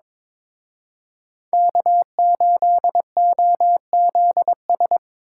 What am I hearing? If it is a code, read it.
Morse code, heard as a keyed tone, about 26 wpm, 700 Hz. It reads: K8OZS
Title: Speed Racing - USA 1x3 Call Signs 22wpm